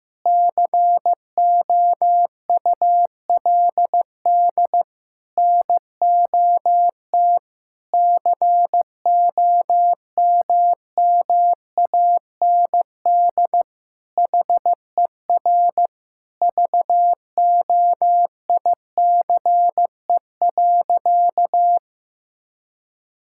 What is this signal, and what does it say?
Morse code, heard as a keyed tone, about 16 wpm, 700 Hz. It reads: COULD NOT COMMAND HER VOICE.
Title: Sentences from Top 1600 Words 15wpm